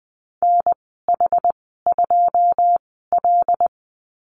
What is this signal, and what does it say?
Morse code, heard as a keyed tone, about 20 wpm, spaced as for 15 wpm, 700 Hz. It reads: NH2L
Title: 082 Lesson 14g - Number 2 - Call Signs 20wpm